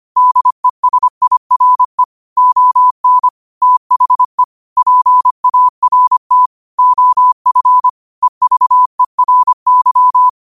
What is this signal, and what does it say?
Morse code, heard as a keyed tone, about 25 wpm, 1000 Hz. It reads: DESIRE ON THE PART OF EVERY